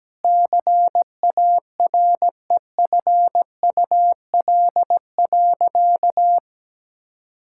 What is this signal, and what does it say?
Morse code, heard as a keyed tone, about 17 wpm, 700 Hz. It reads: CAREFUL.
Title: Sentences from Top 1600 Words 17wpm